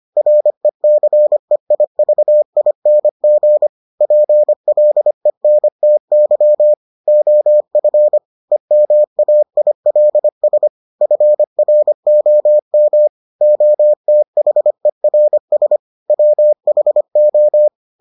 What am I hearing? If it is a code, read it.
Morse code, heard as a keyed tone, about 25 wpm, 600 Hz. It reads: RECEIVING PLENTY OF EMAILS FROM OTHERS WHO